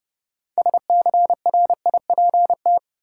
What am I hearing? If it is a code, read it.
Morse code, heard as a keyed tone, about 30 wpm, 700 Hz. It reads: SCRIPT
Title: Six Letter Words - 30wpm